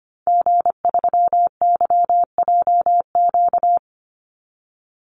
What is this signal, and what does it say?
Morse code, heard as a keyed tone, about 25 wpm, 700 Hz. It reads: G3YJQ
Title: Call Signs Repeated 2x - 25wpm